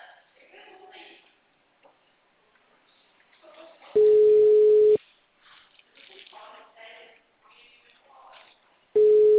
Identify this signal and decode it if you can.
Ringback tone (Europe, 425 Hz), with speech.